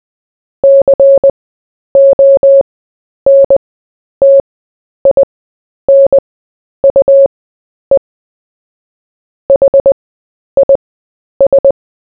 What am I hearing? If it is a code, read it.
Morse code, heard as a keyed tone, about 20 wpm, spaced as for 10 wpm, 550 Hz. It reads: CONTINUE HIS
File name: cw / fists / Short-Story-4-10wpm